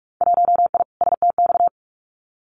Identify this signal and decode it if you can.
Morse code, heard as a keyed tone, about 46 wpm, 700 Hz. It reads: 1I STX
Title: Winter Field Day Exchanges 45wpm